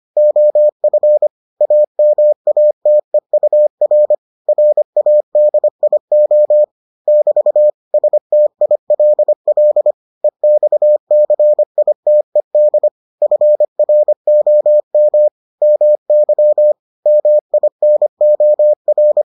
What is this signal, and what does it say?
Morse code, heard as a keyed tone, about 25 wpm, 600 Hz. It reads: OF AMATEUR RADIO <BT> STILL EXCITED FROM MY MINOR